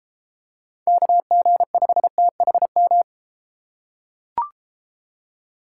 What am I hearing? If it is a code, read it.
Morse code, heard as a keyed tone, about 31 wpm, 700 Hz. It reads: KG5THM